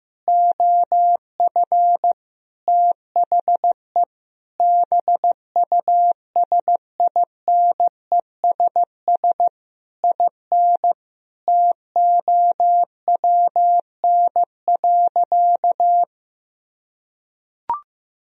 Morse code, heard as a keyed tone, about 15 wpm, 700 Hz. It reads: OF THE BUSINESS IN TOWN.
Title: Sentences from Top 1000 Words - Review Part 2 - 15wpm